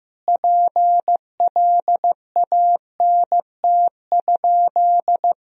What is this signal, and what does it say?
Morse code, heard as a keyed tone, about 16 wpm, 700 Hz. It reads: PLANT?
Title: Sentences from Top 1000 Words - Review Part 1 - 15wpm